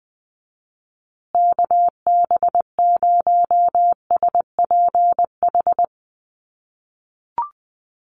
Morse code, heard as a keyed tone, about 20 wpm, 700 Hz. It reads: KB0SPH
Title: USA Call Signs Repeated 2x 20wpm